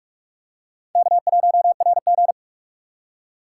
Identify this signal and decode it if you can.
Morse code, heard as a keyed tone, about 44 wpm, 700 Hz. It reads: K1RG